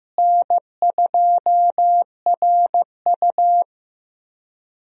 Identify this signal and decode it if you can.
Morse code, heard as a keyed tone, about 16 wpm, 700 Hz. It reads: N2RU
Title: USA Call Signs - 4 Characters - Repeated 3x - 15wpm